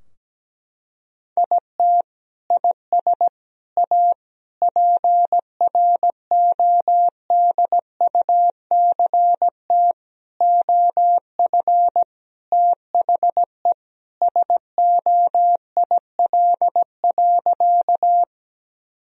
Morse code, heard as a keyed tone, about 17 wpm, 700 Hz. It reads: IT IS A PRODUCT OF THE SOIL.
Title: Sentences from Top 1000 Words - Review Part 4 - 17wpm